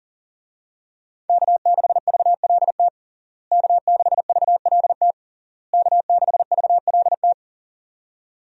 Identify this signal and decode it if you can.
Morse code, heard as a keyed tone, about 37 wpm, 700 Hz. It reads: K6VLT K6VLT K6VLT
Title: USA Call Signs - 5 Characters - Repeated 3x - 40wpm